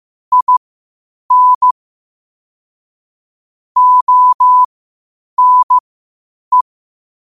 Morse code, heard as a keyed tone, about 15 wpm, spaced as for 8 wpm, 1000 Hz. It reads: IN ONE
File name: qotdfiles / 8wpm